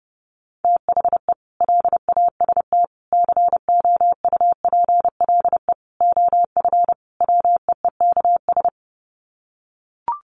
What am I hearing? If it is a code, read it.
Morse code, heard as a keyed tone, about 30 wpm, 700 Hz. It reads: THE LAST COUPLE OF WEEKS
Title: Sets of 5 Words Encore2 30wpm